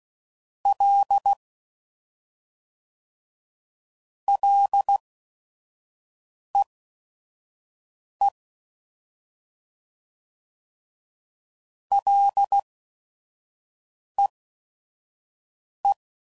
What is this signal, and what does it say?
Morse code, heard as a keyed tone, about 16 wpm, spaced as for 5 wpm, 800 Hz. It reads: L LEE LEE
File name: Prefixes/FLs & THs/Single L's